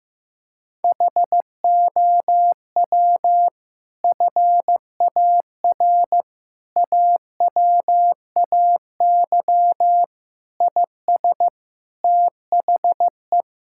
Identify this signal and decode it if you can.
Morse code, heard as a keyed tone, about 15 wpm, 700 Hz. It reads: HOW FAR AWAY IS THE
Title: Sentences from Top 1600 Words 15wpm